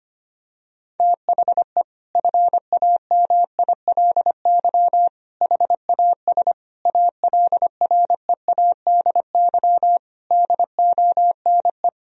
Morse code, heard as a keyed tone, about 25 wpm, 700 Hz. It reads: THE FAMILY HAS ALREADY DONE
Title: Sentences from Top 1000 Words - Review Part 2 - 25wpm